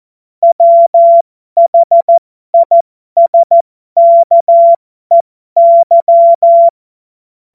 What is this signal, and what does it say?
Morse code, heard as a keyed tone, about 14 wpm, spaced as for 12 wpm, 680 Hz. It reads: WHISKEY
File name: Morsekurs / Witze